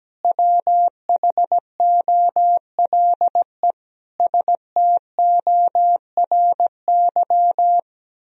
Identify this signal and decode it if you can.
Morse code, heard as a keyed tone, about 17 wpm, 700 Hz. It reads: WHOLE STORY
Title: Sentences from Top 1000 Words - Review Part 4 - 17wpm